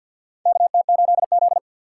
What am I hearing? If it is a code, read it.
Morse code, heard as a keyed tone, about 51 wpm, 700 Hz. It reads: KT8Z
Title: USA Call Signs - 4 Characters - 2x Character Spacing - 50wpm